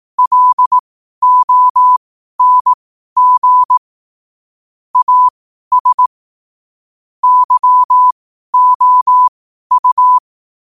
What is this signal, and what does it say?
Morse code, heard as a keyed tone, about 18 wpm, spaced as for 12 wpm, 1000 Hz. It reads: LONG AS YOU
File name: qotdfiles / 12wpm